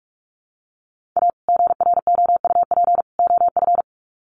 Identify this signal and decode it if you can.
Morse code, heard as a keyed tone, about 44 wpm, 700 Hz. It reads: A GROUP OF
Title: Sets of 3 Words 45wpm